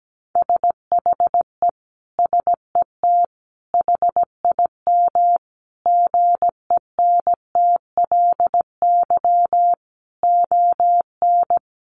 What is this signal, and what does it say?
Morse code, heard as a keyed tone, about 17 wpm, 700 Hz. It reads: SHE SET HIM GENTLY ON
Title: Sentences from Top 1500 Words 17wpm